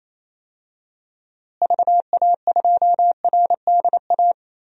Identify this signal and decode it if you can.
Morse code, heard as a keyed tone, about 28 wpm, 700 Hz. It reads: VA2RDA
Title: DXCC-100 Call Signs - 6 Characters - Repeated 3x - 28wpm